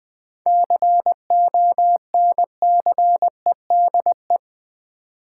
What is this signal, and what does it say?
Morse code, heard as a keyed tone, about 20 wpm, 700 Hz. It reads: CONCEDE
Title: 7 Letter Words - First 3 Letters Given - 20wpm